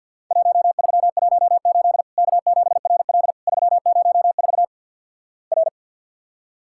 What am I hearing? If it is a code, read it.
Morse code, heard as a keyed tone, about 50 wpm, 700 Hz. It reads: 1318 K7RL 304 R
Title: Worked All Europe DX Contest - QTC Exchange - 50wpm